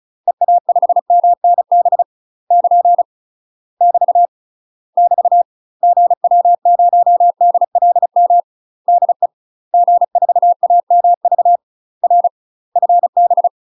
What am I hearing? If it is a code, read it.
Morse code, heard as a keyed tone, about 36 wpm, 700 Hz. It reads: EA5MNB <KN> <BT> <BT> GW0DLM DE G4AMV R FB
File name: UK MASTER QSOs TRAINING / 35WPM / UK MASTER QSOs 0036